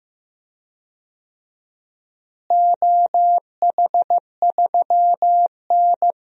Morse code, heard as a keyed tone, about 15 wpm, 700 Hz. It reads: OH3N
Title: DXCC-100 Call Signs - 4 Characters - Repeated 3x - 15wpm